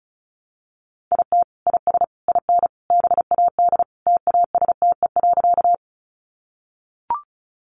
Morse code, heard as a keyed tone, about 34 wpm, 700 Hz. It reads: IT IS IN BAD TASTE.
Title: Sentences from Top 1100 Words 35wpm